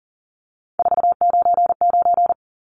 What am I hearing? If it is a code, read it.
Morse code, heard as a keyed tone, about 40 wpm, 700 Hz. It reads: 499
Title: Numbers - 3-Digit 40wpm